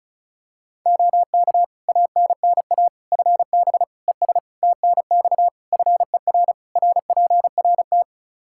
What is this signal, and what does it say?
Morse code, heard as a keyed tone, about 34 wpm, 700 Hz. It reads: OK ANNA FB ES TNX FER RPRT
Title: Practice QSO - Exchange 2 - Info - 35wpm